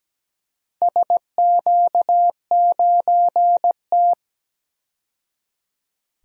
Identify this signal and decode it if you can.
Morse code, heard as a keyed tone, about 17 wpm, 700 Hz. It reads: SQ9T
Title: DXCC-100 Call Signs - 4 Characters - 17wpm